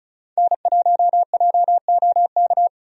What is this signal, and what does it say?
Morse code, heard as a keyed tone, about 36 wpm, 700 Hz. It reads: N1JOK
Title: USA Call Signs - 5 Characters - 35wpm